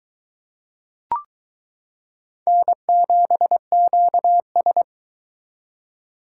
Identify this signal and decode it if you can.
Morse code, heard as a keyed tone, about 22 wpm, 700 Hz. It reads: N7QS